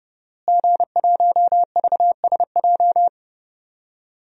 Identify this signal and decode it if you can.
Morse code, heard as a keyed tone, about 30 wpm, 700 Hz. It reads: G1VSJ
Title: DXCC-100 Call Signs - 5 Characters - Repeated 2x - 30wpm